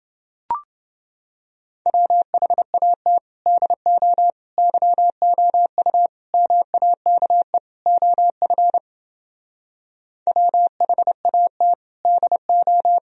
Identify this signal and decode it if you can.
Morse code, heard as a keyed tone, about 29 wpm, 700 Hz. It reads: WHAT DO YOU MAKE OF WHAT DO